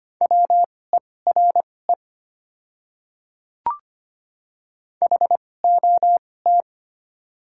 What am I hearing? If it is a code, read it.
Morse code, heard as a keyed tone, about 25 wpm, spaced as for 18 wpm, 700 Hz. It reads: WERE HOT
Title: 2x Character Spacing - Top 500 Words - 25wpm